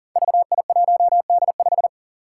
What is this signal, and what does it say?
Morse code, heard as a keyed tone, about 41 wpm, 700 Hz. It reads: VI1D5
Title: Five Challenging Letter-Number Combinations 40wpm